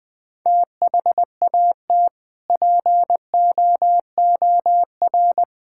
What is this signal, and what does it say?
Morse code, heard as a keyed tone, about 20 wpm, 700 Hz. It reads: THAT POOR
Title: Sentences from Top 1000 Words - Review Part 5 - 20wpm